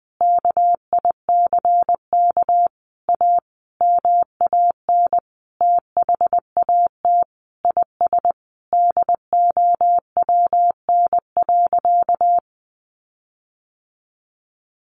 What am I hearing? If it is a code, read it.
Morse code, heard as a keyed tone, about 20 wpm, 700 Hz. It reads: KICK A MAN THAT IS DOWN.
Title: Sentences from Top 1300 Words 20wpm